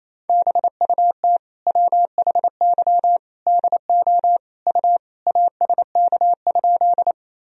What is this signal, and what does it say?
Morse code, heard as a keyed tone, about 27 wpm, 700 Hz. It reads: BUT WHY DO U ASK?
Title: Sentences from Top 200 Words and Abbreviations 28wpm